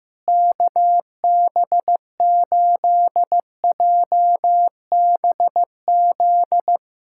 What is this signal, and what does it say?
Morse code, heard as a keyed tone, about 15 wpm, 700 Hz. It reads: KB8JBZ
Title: USA Callsign DE Callsign 15wpm